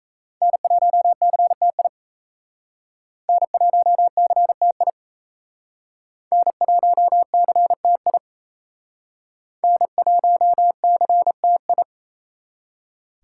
Morse code, heard as a keyed tone, about 35 wpm, 700 Hz. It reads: N1CTI N1CTI N1CTI N1CTI
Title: Speed Racing - USA Call Signs 28wpm